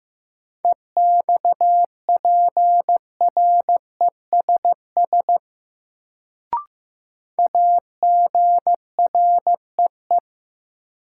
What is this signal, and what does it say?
Morse code, heard as a keyed tone, about 15 wpm, 700 Hz. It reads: EXPRESS AGREE
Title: Sentences from Top 1100 Words - New Word Drill Alt - 15wpm